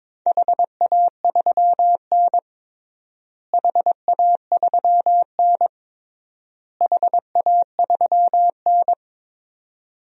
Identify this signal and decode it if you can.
Morse code, heard as a keyed tone, about 21 wpm, 700 Hz. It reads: HA3N HA3N HA3N
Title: DXCC-100 Call Signs - 4 Characters - Repeated 3x - 22wpm